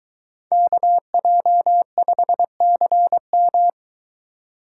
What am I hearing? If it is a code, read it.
Morse code, heard as a keyed tone, about 22 wpm, 700 Hz. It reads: KJ5CM